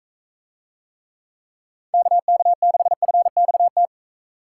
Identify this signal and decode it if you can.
Morse code, heard as a keyed tone, about 40 wpm, 700 Hz. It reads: KK6FXT